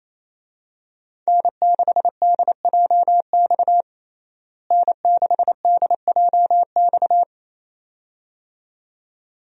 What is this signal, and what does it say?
Morse code, heard as a keyed tone, about 28 wpm, 700 Hz. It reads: N6DJX N6DJX
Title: USA Call Signs - 5 Characters - Repeated 2x - 28wpm